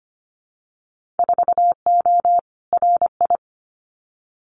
Morse code, heard as a keyed tone, about 25 wpm, 700 Hz. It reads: 4O RI
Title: Winter Field Day Exchanges Encore 25wpm